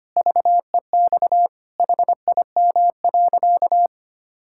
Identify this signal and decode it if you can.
Morse code, heard as a keyed tone, about 25 wpm, 700 Hz. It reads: VEX HIM.
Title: Pangram Challenge v2 - 25wpm